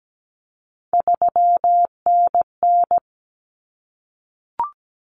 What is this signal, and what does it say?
Morse code, heard as a keyed tone, about 17 wpm, 700 Hz. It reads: 3NN
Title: Common Cut Numbers - 3-Digit 17wpm